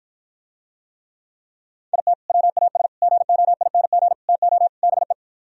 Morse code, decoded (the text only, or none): IT WAS GOING TO BE